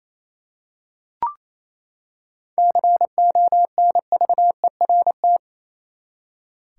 Morse code, heard as a keyed tone, about 28 wpm, 700 Hz. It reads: CONVERT
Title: Seven Letter Words - 28wpm